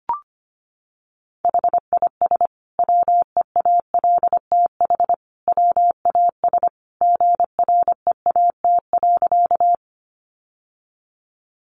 Morse code, heard as a keyed tone, about 25 wpm, 700 Hz. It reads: HIS WEALTH WAS GREAT.
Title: Sentences from Top 1200 Words 25wpm